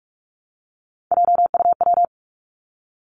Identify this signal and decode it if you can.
Morse code, heard as a keyed tone, about 46 wpm, 700 Hz. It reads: JUW
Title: Three Challenging Letter-Number Combinations 45wpm